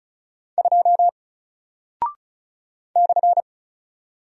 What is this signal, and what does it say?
Morse code, heard as a keyed tone, about 36 wpm, 700 Hz. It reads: 2 /
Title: Single Character (Letter:Number:Punctuation:Prosign) 35wpm